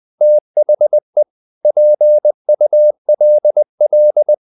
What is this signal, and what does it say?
Morse code, heard as a keyed tone, about 20 wpm, 600 Hz. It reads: THE PULL
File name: the old man and the sea part 2 20wpm